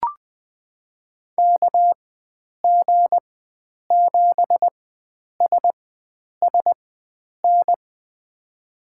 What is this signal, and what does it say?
Morse code, heard as a keyed tone, about 20 wpm, spaced as for 9 wpm, 700 Hz. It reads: KG7SSN
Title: USA Call Signs - 6 Characters - 4x Character Spacing - 20wpm